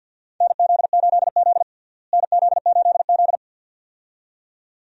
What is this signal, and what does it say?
Morse code, heard as a keyed tone, about 47 wpm, 690 Hz. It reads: NZ8Z NZ8Z
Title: USA Call Signs - 4 Characters - Repeated 2x - 50wpm